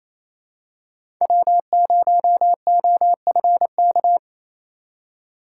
Morse code, heard as a keyed tone, about 28 wpm, 700 Hz. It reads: W0OFK
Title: USA Call Signs - 5 Characters - 4x Character Spacing - 28wpm